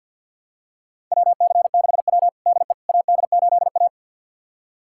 Morse code, heard as a keyed tone, about 49 wpm, 700 Hz. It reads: WC6W DE AD8A
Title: USA Callsign DE Callsign - 4 Characters - 50wpm